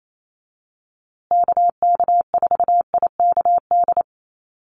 Morse code, heard as a keyed tone, about 28 wpm, 700 Hz. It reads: KK4IKD
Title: USA Call Signs Repeated 2x 28wpm